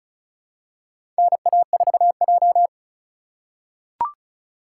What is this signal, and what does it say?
Morse code, heard as a keyed tone, about 36 wpm, 700 Hz. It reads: NA4J
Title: USA Call Signs - 3x Character Spacing - 35wpm